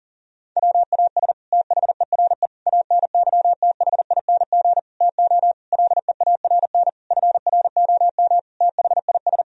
Morse code, heard as a keyed tone, about 37 wpm, 700 Hz. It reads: WAS THERE ANYTHING TO LEARN FROM THIS
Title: Sentences from Top 1000 Words - Review Part 1 - 40wpm